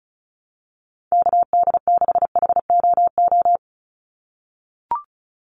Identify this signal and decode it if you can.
Morse code, heard as a keyed tone, about 34 wpm, 700 Hz. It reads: KD6HOO
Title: USA Call Signs 35wpm